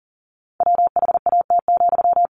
Morse code, heard as a keyed tone, about 40 wpm, 700 Hz. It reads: WHAT,
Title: Sentences from Top 1300 Words 40wpm